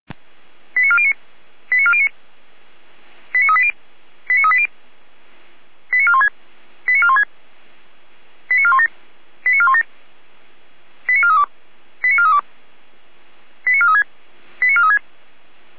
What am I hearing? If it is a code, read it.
Five-tone selective call (ZVEI), heard as quick runs of five tones, tones of about 70 ms. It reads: 89409 89409 89380 89380 89416 89416 89417 89417 89432 89432 89436 89436